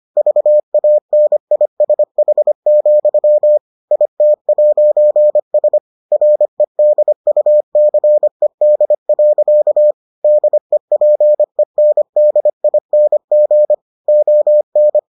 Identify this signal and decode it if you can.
Morse code, heard as a keyed tone, about 25 wpm, 600 Hz. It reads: VANISH, IT'S REDUCED. DEPENDING ON